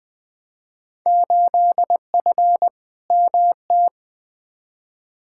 Morse code, heard as a keyed tone, about 20 wpm, 700 Hz. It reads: 8F MT
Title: Field Day Exchanges Encore - 20wpm